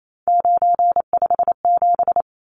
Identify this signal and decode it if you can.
Morse code, heard as a keyed tone, about 28 wpm, 700 Hz. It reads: 957
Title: Numbers - 3-Digit 28wpm